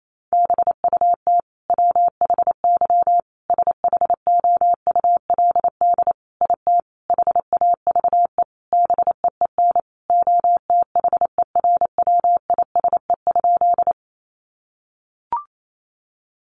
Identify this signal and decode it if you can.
Morse code, heard as a keyed tone, about 28 wpm, 700 Hz. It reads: BUT WHY SHOULD IT HAVE BEEN OTHERWISE?
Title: Sentences from Top 1200 Words 28wpm